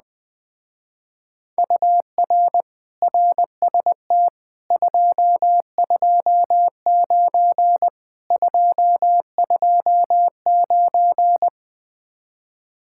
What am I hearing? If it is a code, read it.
Morse code, heard as a keyed tone, about 20 wpm, 700 Hz. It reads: UR RST 229 229
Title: Speed-Racing Common QSO Phrases Encore - 17wpm